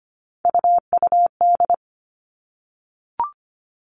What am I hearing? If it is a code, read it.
Morse code, heard as a keyed tone, about 25 wpm, 700 Hz. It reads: UUD
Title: All Cut Numbers - 3-Digit 25wpm